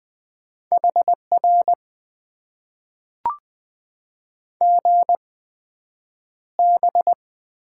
Morse code, heard as a keyed tone, about 20 wpm, spaced as for 9 wpm, 700 Hz. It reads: HR GB
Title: Top 90 QSO Elements - 8x Character Spacing - 20wpm